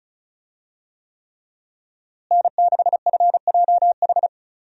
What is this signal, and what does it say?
Morse code, heard as a keyed tone, about 36 wpm, 700 Hz. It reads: N6FJH
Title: USA Call Signs - 5 Characters - Repeated 3x - 35wpm